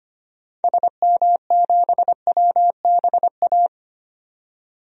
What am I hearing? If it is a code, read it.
Morse code, heard as a keyed tone, about 25 wpm, 700 Hz. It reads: SM7WBA
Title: DXCC-100 Call Signs - 6 Characters - 25wpm